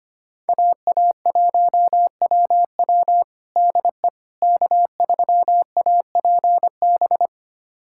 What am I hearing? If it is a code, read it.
Morse code, heard as a keyed tone, about 25 wpm, 700 Hz. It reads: AA1WW DE K3APB
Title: USA Callsign DE Callsign - 5 Characters - 25wpm